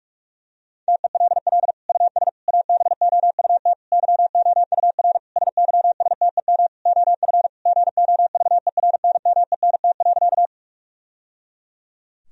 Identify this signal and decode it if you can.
Morse code, heard as a keyed tone, about 46 wpm, 700 Hz. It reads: TELL US ABOUT YOUR SYSTEM OF GOVERNMENT.
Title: Sentences from Top 1000 Words - Review Part 3 - 45wpm